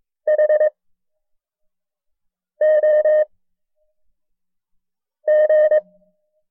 Morse code, heard as a keyed tone, about 22 wpm, spaced as for 4 wpm, 610 Hz. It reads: HOG
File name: FreeFiles / CW-practice / KB6MT-tape1-side2